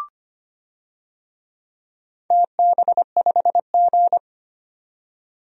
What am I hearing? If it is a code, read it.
Morse code, heard as a keyed tone, about 25 wpm, 700 Hz. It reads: TB5G
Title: DXCC-100 Call Signs - 4 Characters - 25wpm